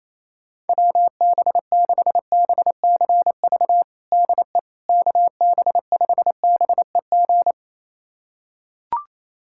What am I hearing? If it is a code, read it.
Morse code, heard as a keyed tone, about 28 wpm, 700 Hz. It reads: WB6BCV DE KB5BEG
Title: USA Callsign DE Callsign - 6 Characters - 28wpm